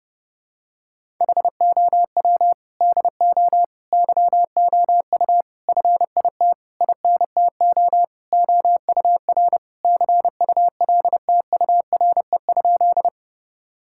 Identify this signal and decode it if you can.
Morse code, heard as a keyed tone, about 30 wpm, 700 Hz. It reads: HOW DO YOU FIT INTO OUR CULTURE?
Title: Sentences from Top 1000 Words - Review Part 6 - 30wpm